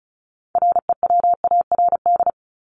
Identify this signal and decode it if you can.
Morse code, heard as a keyed tone, about 34 wpm, 700 Hz. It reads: REWARD
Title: Top 1751-2000 Words 35wpm